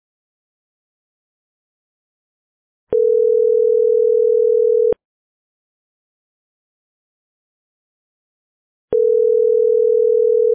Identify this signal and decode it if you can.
Ringback tone (North America), 440 + 480 Hz.